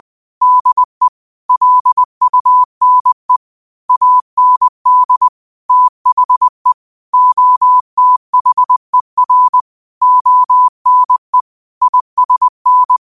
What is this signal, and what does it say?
Morse code, heard as a keyed tone, about 20 wpm, 1000 Hz. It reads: DE LUNE AND THE OTHER ONE ISN